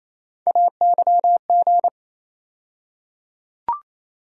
Morse code, heard as a keyed tone, about 28 wpm, 700 Hz. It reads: AYG
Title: USA Call Sign Postfixes - 3 Letters 28wpm